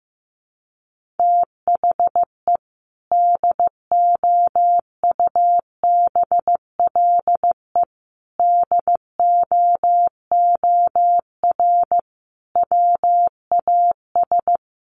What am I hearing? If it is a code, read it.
Morse code, heard as a keyed tone, about 15 wpm, 700 Hz. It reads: THE DOUBLE DOOR WAS
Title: Sentences from Top 1400 Words 15wpm